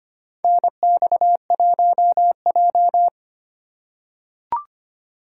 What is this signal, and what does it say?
Morse code, heard as a keyed tone, about 25 wpm, 700 Hz. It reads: NX1J
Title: USA Call Signs - 2x Character Spacing - 25wpm